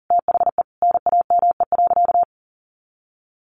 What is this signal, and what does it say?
Morse code, heard as a keyed tone, about 40 wpm, 700 Hz. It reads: THE NAME.
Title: Sentences from Top 1100 Words 40wpm